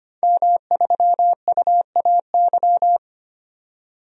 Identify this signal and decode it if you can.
Morse code, heard as a keyed tone, about 25 wpm, 700 Hz. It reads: M3UAY
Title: DXCC-100 Call Signs - 5 Characters - 25wpm